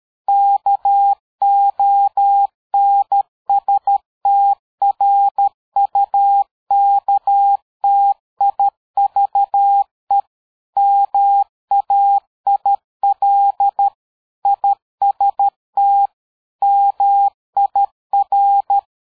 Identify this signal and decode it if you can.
Morse code, heard as a keyed tone, about 13 wpm, 800 Hz. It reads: KONSTRUKTIVE MAIL IST MIR
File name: dk5ke / PersWorte 60